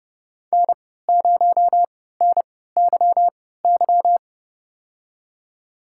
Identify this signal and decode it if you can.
Morse code, heard as a keyed tone, about 30 wpm, spaced as for 17 wpm, 700 Hz. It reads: N0NYY
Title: USA Call Signs - 5 Characters - 3x Character Spacing - 30wpm